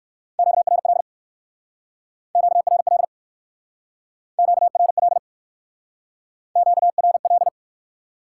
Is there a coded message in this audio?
Yes: Morse code from 0.4 to 7.5 s, heard as a keyed tone, about 61 wpm, 700 Hz. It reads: QRL QRL QRL QRL